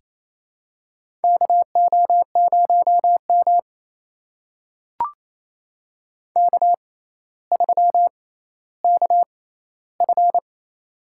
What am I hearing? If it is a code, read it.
Morse code, heard as a keyed tone, about 28 wpm, spaced as for 14 wpm, 700 Hz. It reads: KO0M K3KF
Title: USA Call Signs - 4 Characters - 6x Character Spacing - 28wpm